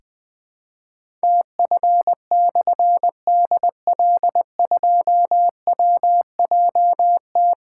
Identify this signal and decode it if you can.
Morse code, heard as a keyed tone, about 20 wpm, 700 Hz. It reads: TF/DL2WJT
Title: Difficult Call Signs 20wpm